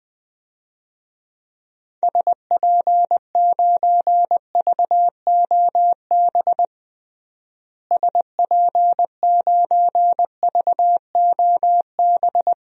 Morse code, heard as a keyed tone, about 20 wpm, 700 Hz. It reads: SP9VOB SP9VOB
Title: DXCC-100 Call Signs - 6 Characters - Repeated 2x - 20wpm